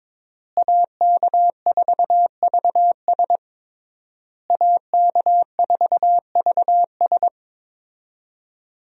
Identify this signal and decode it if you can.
Morse code, heard as a keyed tone, about 22 wpm, 700 Hz. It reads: AK4VS AK4VS
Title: USA Call Signs - 5 Characters - Repeated 3x - 22wpm